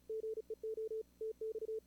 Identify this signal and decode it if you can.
Morse code, heard as a keyed tone, about 35 wpm, 440 Hz. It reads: GEO TX
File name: CWSprint / Feb2022 / k5tr-feb-na-sprint-cw-k5ot -2300